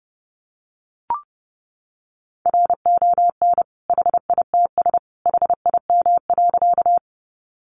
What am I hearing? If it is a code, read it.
Morse code, heard as a keyed tone, about 30 wpm, 700 Hz. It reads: RON HITS HIM.
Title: Additional Easy Phrases 30wpm